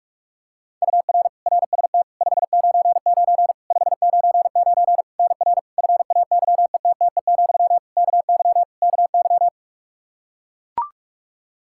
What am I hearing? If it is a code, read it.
Morse code, heard as a keyed tone, about 42 wpm, 700 Hz. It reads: UR RST 599 599 NR FAYETTE, KY KY
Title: Practice QSO - Exchange 1 - Beginning - 45wpm